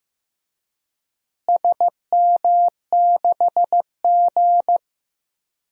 Morse code, heard as a keyed tone, about 15 wpm, 700 Hz. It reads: SM6G
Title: DXCC-100 Call Signs - 4 Characters - Repeated 3x - 15wpm